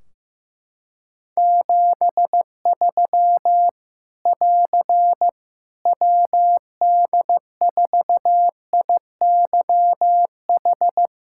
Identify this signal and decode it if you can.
Morse code, heard as a keyed tone, about 15 wpm, 700 Hz. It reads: 73 <AR> WD4IYH
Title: K1USN SST QSO Practice 15wpm